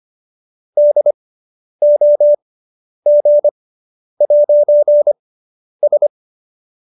Morse code, heard as a keyed tone, about 25 wpm, spaced as for 10 wpm, 590 Hz. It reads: DOG'S